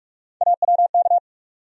Morse code, heard as a keyed tone, about 42 wpm, 700 Hz. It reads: AWK
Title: USA Call Sign Postfixes - 3 Letters 45wpm